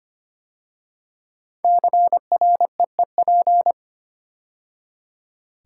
Morse code, heard as a keyed tone, about 25 wpm, 700 Hz. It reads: CREEP
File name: 5 Letter Words - First 3 Letters Given - 25wpm